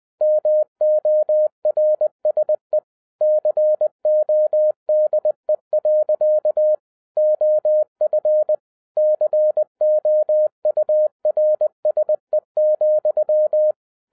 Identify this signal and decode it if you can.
Morse code, heard as a keyed tone, about 20 wpm, 610 Hz. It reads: MORSE CODE. OF COURSE,